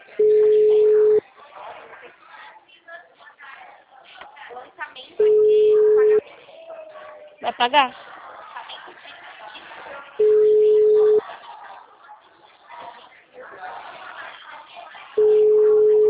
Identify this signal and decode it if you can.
Ringback tone (Europe, 425 Hz), with speech.